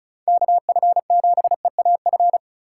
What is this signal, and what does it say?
Morse code, heard as a keyed tone, about 36 wpm, 700 Hz. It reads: KF7EAF
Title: USA Call Signs - 6 Characters - 4x Character Spacing - 35wpm